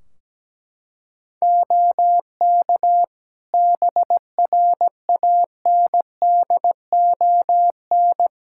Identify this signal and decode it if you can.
Morse code, heard as a keyed tone, about 17 wpm, 700 Hz. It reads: OK BRANDON
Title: K1USN SST QSO Practice 17wpm